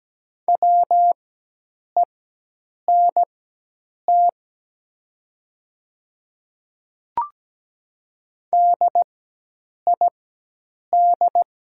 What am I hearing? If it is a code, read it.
Morse code, heard as a keyed tone, about 17 wpm, spaced as for 8 wpm, 690 Hz. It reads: WENT DID
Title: 4x Character Spacing - Top 300 Words - 17wpm